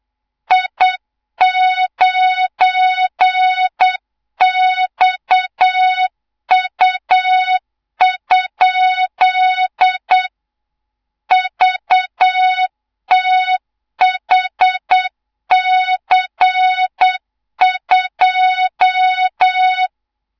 Morse code, heard as a keyed tone, about 8 wpm, 3700 Hz, with noise. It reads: I9XU? VTHC2
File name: Morze-HA2MN / hang / 24 GYAKORLAT